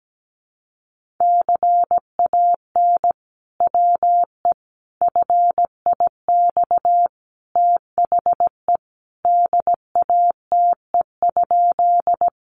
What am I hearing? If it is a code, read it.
Morse code, heard as a keyed tone, about 17 wpm, 700 Hz. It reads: CAN WE FIX THE DATE?
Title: Sentences from Top 1400 Words 17wpm